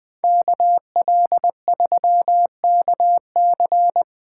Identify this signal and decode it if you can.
Morse code, heard as a keyed tone, about 20 wpm, 700 Hz. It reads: KL3KC
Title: Speed Racing - USA 2x2 Call Signs 17wpm